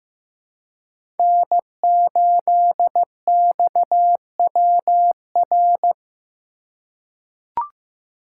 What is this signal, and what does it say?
Morse code, heard as a keyed tone, about 15 wpm, 700 Hz. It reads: N8XWR
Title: USA Call Signs - 5 Characters - 2x Character Spacing - 15wpm